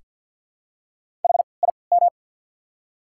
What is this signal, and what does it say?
Morse code, heard as a keyed tone, about 53 wpm, spaced as for 29 wpm, 690 Hz. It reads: HIM
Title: 3x Character Spacing - Top 300 Words - 50wpm